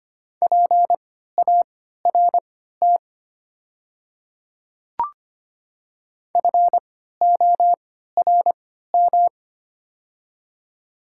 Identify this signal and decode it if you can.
Morse code, heard as a keyed tone, about 25 wpm, spaced as for 14 wpm, 700 Hz. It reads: PART FORM
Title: 3x Character Spacing - Top 200 Words - 25wpm